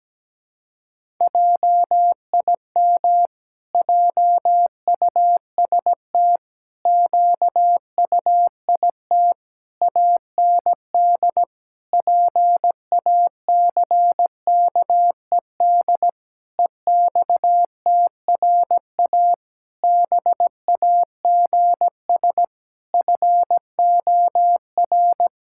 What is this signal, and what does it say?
Morse code, heard as a keyed tone, about 17 wpm, 700 Hz. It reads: JIM JUST QUIT AND PACKED EXTRA BAGS FOR
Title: Pangram Challenge v2 - 17wpm